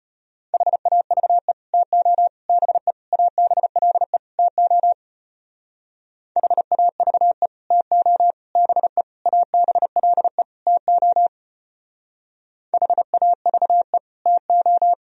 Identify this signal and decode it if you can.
Morse code, heard as a keyed tone, about 31 wpm, 700 Hz. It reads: HAVE TO BE ABLE TO HAVE TO BE ABLE TO HAVE TO
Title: Speed-Racing - Sets of 5 Words 25wpm